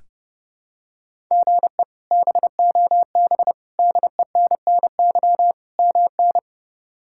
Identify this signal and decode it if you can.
Morse code, heard as a keyed tone, about 30 wpm, 700 Hz. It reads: GE BOB DENNY MN
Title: K1USN SST QSO Practice - 30wpm